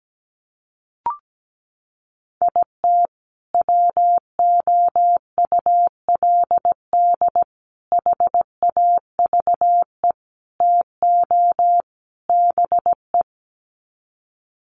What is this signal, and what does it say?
Morse code, heard as a keyed tone, about 17 wpm, 700 Hz. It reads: IT WOULD HAVE TO BE
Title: Sets of 5 Words Encore 17wpm